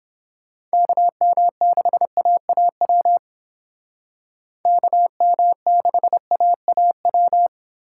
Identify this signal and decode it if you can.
Morse code, heard as a keyed tone, about 29 wpm, 700 Hz. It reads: KM6AAW KM6AAW